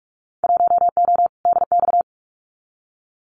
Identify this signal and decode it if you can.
Morse code, heard as a keyed tone, about 46 wpm, 700 Hz. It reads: 1O DX
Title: Winter Field Day Exchanges 45wpm